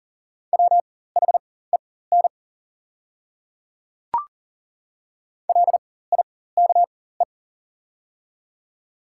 Morse code, heard as a keyed tone, about 40 wpm, spaced as for 19 wpm, 700 Hz. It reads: WHEN LIKE
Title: 4x Character Spacing - Top 100 Words - 40wpm